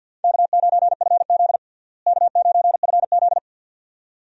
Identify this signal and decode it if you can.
Morse code, heard as a keyed tone, about 48 wpm, 700 Hz. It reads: K9FZ K9FZ
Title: USA Call Signs - 4 Characters - Repeated 2x - 50wpm